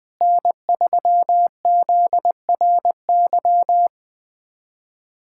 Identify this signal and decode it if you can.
Morse code, heard as a keyed tone, about 20 wpm, 700 Hz. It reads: N3ZRY
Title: USA Callsign DE Callsign 20wpm